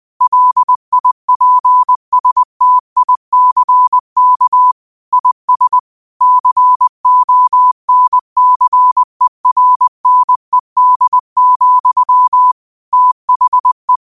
Morse code, heard as a keyed tone, about 20 wpm, 1000 Hz. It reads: LIPSTICK IS CONCERNED, THE
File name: qotdfiles / 20wpm